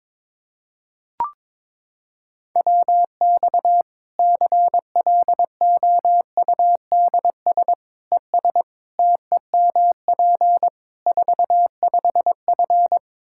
Morse code, heard as a keyed tone, about 22 wpm, 700 Hz. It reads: WX CLOUDS ES TEMP 45F
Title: Practice QSO - Exchange 2 - Info - 22wpm